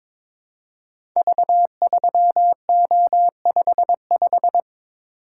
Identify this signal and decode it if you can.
Morse code, heard as a keyed tone, about 22 wpm, 700 Hz. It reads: V3O55
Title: Code Groups - 5 Letter:Numbers - 22wpm